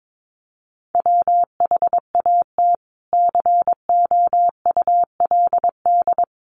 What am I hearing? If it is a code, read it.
Morse code, heard as a keyed tone, about 22 wpm, 700 Hz. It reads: WHAT COULD
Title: Sentences from Top 200 Words 22wpm